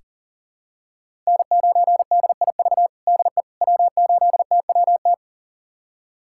Morse code, heard as a keyed tone, about 40 wpm, 700 Hz. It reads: N9DIV DE W8TWT
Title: USA Callsign DE Callsign - 5 Characters - 40wpm